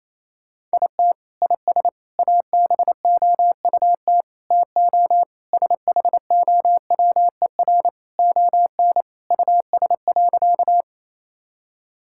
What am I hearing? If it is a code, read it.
Morse code, heard as a keyed tone, about 28 wpm, 700 Hz. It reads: IT IS ABOUT TO SHOWER ON US.
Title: Sentences from Top 1600 Words 28wpm